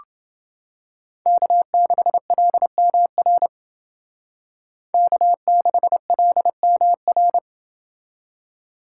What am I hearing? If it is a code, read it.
Morse code, heard as a keyed tone, about 29 wpm, 700 Hz. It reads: K6LMR K6LMR